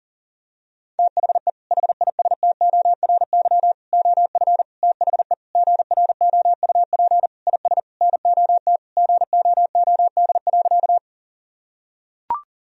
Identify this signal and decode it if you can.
Morse code, heard as a keyed tone, about 40 wpm, 700 Hz. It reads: THE HISTORY OF THE GROUP IS NOT GOOD.
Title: Sentences from Top 1000 Words - Review Part 1 - 40wpm